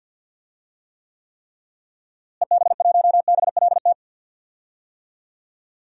Morse code, heard as a keyed tone, about 48 wpm, 700 Hz. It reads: EB1BLT
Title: DXCC-100 Call Signs - 6 Characters - Repeated 3x - 50wpm